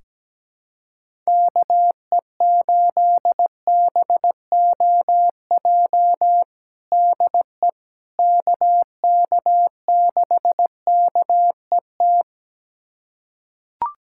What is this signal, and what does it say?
Morse code, heard as a keyed tone, about 17 wpm, 700 Hz. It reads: KE8BOJ DE KK6KET
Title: USA Callsign DE Callsign - 6 Characters - 17wpm